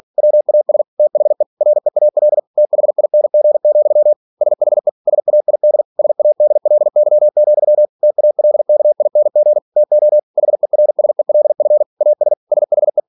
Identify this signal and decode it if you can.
Morse code, heard as a keyed tone, about 46 wpm, 600 Hz. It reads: WAS THE REAL THING, SHE SAID SADLY, TALKING TO HERSELF AS SHE
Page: Never 47wpm